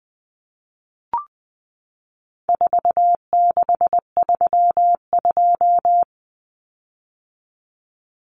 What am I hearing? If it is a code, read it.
Morse code, heard as a keyed tone, about 20 wpm, 700 Hz. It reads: 4632
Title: Common Cut Numbers - 4-Digit 20wpm